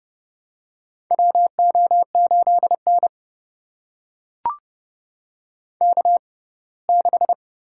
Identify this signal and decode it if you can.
Morse code, heard as a keyed tone, about 30 wpm, spaced as for 20 wpm, 700 Hz. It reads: WO8N K6
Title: USA Call Signs - 4 Characters - 6x Character Spacing - 30wpm